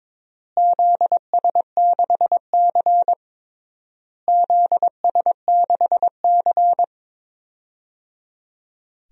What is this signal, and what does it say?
Morse code, heard as a keyed tone, about 21 wpm, 700 Hz. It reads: ZS6C ZS6C
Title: DXCC-100 Call Signs - 4 Characters - Repeated 3x - 22wpm